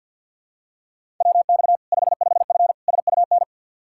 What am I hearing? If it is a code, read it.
Morse code, heard as a keyed tone, about 46 wpm, 700 Hz. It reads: WX 55F SUN
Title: Practice QSO - Exchange 2 - Info - 50wpm